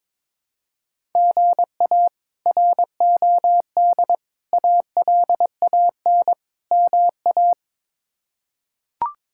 Morse code, heard as a keyed tone, about 22 wpm, 700 Hz. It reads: GA ROD ALAN MA
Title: K1USN SST QSO Practice - 22wpm